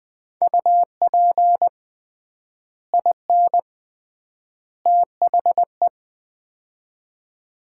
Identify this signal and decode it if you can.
Morse code, heard as a keyed tone, about 20 wpm, 700 Hz. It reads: UP IN THE
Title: 3x Word-Spacing - Sets of 3 Words - 20wpm